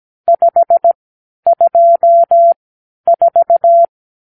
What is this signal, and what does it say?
Morse code, heard as a keyed tone, about 17 wpm, spaced as for 10 wpm, 680 Hz. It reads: 524
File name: arrl / Review (1-5)